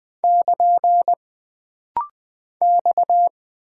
Morse code, heard as a keyed tone, about 20 wpm, 700 Hz. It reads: <KN> X
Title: Single Character (Letter:Number:Punctuation:Prosign) - Rapid-Fire - 20wpm